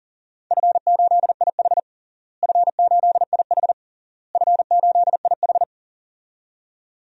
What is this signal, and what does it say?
Morse code, heard as a keyed tone, about 40 wpm, 700 Hz. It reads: F8IH F8IH F8IH
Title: DXCC-100 Call Signs - 4 Characters - Repeated 3x - 40wpm